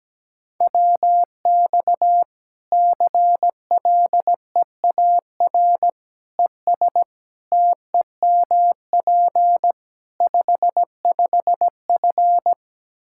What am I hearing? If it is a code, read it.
Morse code, heard as a keyed tone, about 17 wpm, 700 Hz. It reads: WX CLEAR ES TEMP 55F
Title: Practice QSO - Exchange 2 - Info - 17wpm